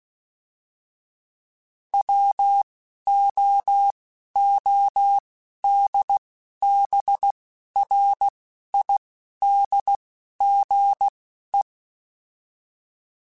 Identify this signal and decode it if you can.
Morse code, heard as a keyed tone, about 16 wpm, spaced as for 11 wpm, 800 Hz. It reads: WOODBRIDGE